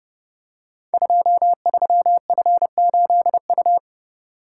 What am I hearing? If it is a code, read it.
Morse code, heard as a keyed tone, about 30 wpm, 700 Hz. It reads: 23F8U
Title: Code Groups - 5 Letter:Numbers - 30wpm